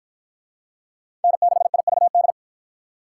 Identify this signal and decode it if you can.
Morse code, heard as a keyed tone, about 53 wpm, 700 Hz. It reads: N6IVD